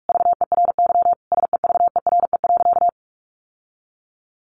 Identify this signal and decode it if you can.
Morse code, heard as a keyed tone, about 44 wpm, 700 Hz. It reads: VERY SEVERE.
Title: Sentences from Top 1400 Words 45wpm